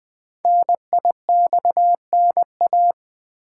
Morse code, heard as a keyed tone, about 20 wpm, 700 Hz. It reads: NIXNA
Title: Code Groups - 5 Letters - 20wpm